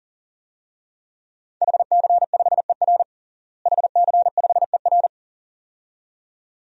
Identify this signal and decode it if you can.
Morse code, heard as a keyed tone, about 37 wpm, 700 Hz. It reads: HC5ER HC5ER
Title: DXCC-100 Call Signs - 5 Characters - Repeated 2x - 40wpm